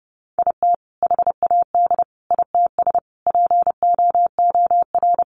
Morse code, heard as a keyed tone, about 28 wpm, 700 Hz. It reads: IT HAD ITS POOR
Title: Sentences from Top 1000 Words 30wpm